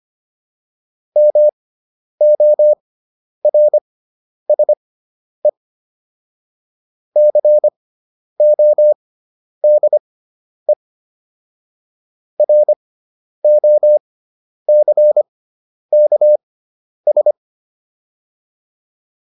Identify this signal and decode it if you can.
Morse code, heard as a keyed tone, about 25 wpm, spaced as for 10 wpm, 600 Hz. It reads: MORSE CODE ROCKS